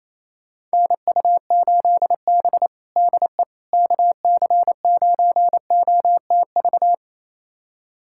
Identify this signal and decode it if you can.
Morse code, heard as a keyed tone, about 28 wpm, 700 Hz. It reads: NU8B DE KC9OTV
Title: USA Callsign DE Callsign 28wpm